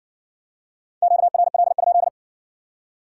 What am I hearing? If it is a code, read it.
Morse code, heard as a keyed tone, about 60 wpm, 700 Hz. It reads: QRL?